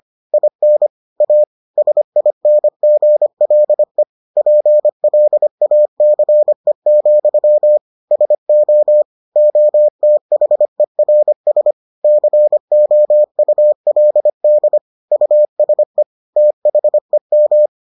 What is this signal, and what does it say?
Morse code, heard as a keyed tone, about 25 wpm, 600 Hz. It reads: IN A SINGLE PLACE, SO OTHERS COULD USE THEM